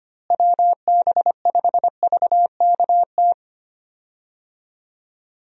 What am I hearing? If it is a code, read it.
Morse code, heard as a keyed tone, about 25 wpm, 700 Hz. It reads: WB5VKT
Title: USA Call Signs - 6 Characters - Repeated 3x - 25wpm